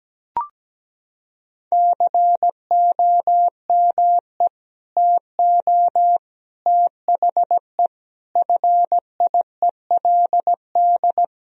Morse code, heard as a keyed tone, about 17 wpm, 700 Hz. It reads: COME TO THE FIELD
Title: Sentences from Top 1000 Words - Review Part 2 - 17wpm